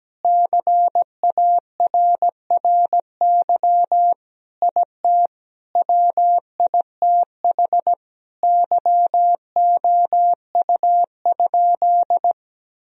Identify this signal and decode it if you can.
Morse code, heard as a keyed tone, about 17 wpm, 700 Hz. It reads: CARRY IT WITH YOU?
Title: Sentences from Top 1000 Words - Review Part 1 - 17wpm